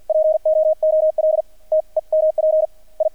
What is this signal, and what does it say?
Morse code, heard as a keyed tone, about 50 wpm, spaced as for 43 wpm, 630 Hz. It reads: POOL TEMP I
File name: archived / CRSnet